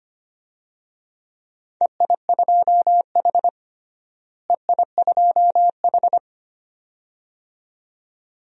Morse code, heard as a keyed tone, about 25 wpm, 700 Hz. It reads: EI2H EI2H
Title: DXCC-100 Call Signs - 4 Characters - Repeated 2x - 25wpm